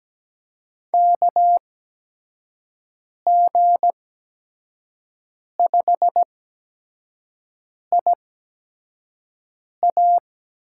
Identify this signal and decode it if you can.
Morse code, heard as a keyed tone, about 17 wpm, spaced as for 5 wpm, 700 Hz. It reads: KG5IA
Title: USA Call Signs - 6 Characters - 8x Character Spacing - 17wpm